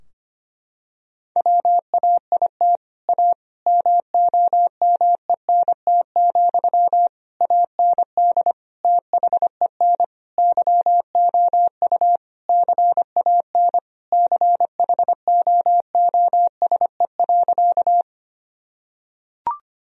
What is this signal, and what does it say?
Morse code, heard as a keyed tone, about 25 wpm, 700 Hz. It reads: WAIT A MOMENT, AND THEN YOU CAN CHOOSE.
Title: Sentences from Top 1000 Words - Review Part 4 - 25wpm